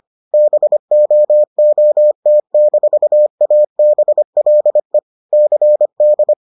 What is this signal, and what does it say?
Morse code, heard as a keyed tone, about 25 wpm, 600 Hz. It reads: BOOT-ABLE CD